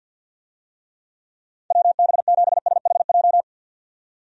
Morse code, heard as a keyed tone, about 50 wpm, 700 Hz. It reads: WB7SHJ
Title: USA Call Signs - 6 Characters - 50wpm